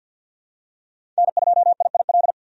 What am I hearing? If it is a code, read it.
Morse code, heard as a keyed tone, about 53 wpm, 700 Hz. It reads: N2IIL
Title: USA Call Signs - 5 Characters - 50wpm